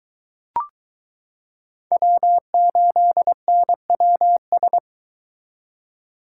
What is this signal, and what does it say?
Morse code, heard as a keyed tone, about 22 wpm, 700 Hz. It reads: W8NWS